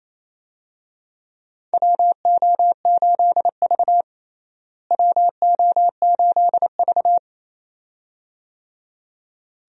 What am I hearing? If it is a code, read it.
Morse code, heard as a keyed tone, about 28 wpm, 700 Hz. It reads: WO8V WO8V
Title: USA Call Signs - 4 Characters - Repeated 2x - 28wpm